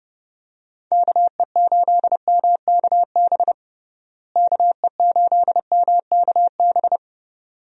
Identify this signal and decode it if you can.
Morse code, heard as a keyed tone, about 32 wpm, 700 Hz. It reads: KE8MKB KE8MKB
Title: USA Call Signs - 6 Characters - Repeated 2x - 30wpm